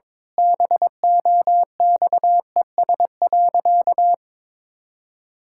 Morse code, heard as a keyed tone, about 22 wpm, 700 Hz. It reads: BOXES.
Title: Pangram Challenge v2 - 22wpm